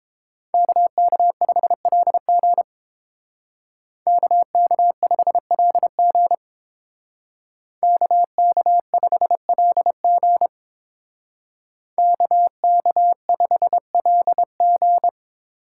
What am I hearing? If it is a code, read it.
Morse code, heard as a keyed tone, about 27 wpm, 700 Hz. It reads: KK5LG KK5LG KK5LG KK5LG
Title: Speed Racing - USA 2x2 Call Signs 22wpm